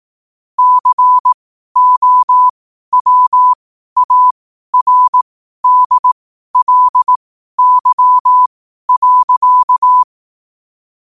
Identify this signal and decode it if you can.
Morse code, heard as a keyed tone, about 18 wpm, spaced as for 12 wpm, 1000 Hz. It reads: COWARDLY.
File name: qotdfiles / 12wpm